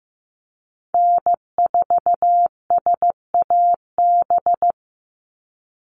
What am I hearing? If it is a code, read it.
Morse code, heard as a keyed tone, about 15 wpm, 700 Hz. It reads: N4SAB
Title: USA Call Signs 15wpm